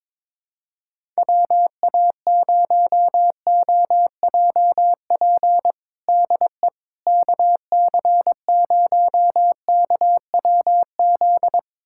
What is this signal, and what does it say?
Morse code, heard as a keyed tone, about 22 wpm, 700 Hz. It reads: WA0OJP DE KC0KWZ
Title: USA Callsign DE Callsign - 6 Characters - 22wpm